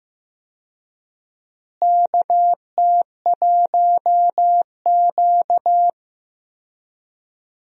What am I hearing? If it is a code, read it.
Morse code, heard as a keyed tone, about 15 wpm, 700 Hz. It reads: KT1Q
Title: USA Call Signs - 4 Characters - Repeated 3x - 15wpm